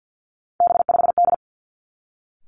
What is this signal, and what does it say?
Morse code, heard as a keyed tone, about 50 wpm, 700 Hz. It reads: B5D
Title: Three Challenging Letter-Number Combinations 50wpm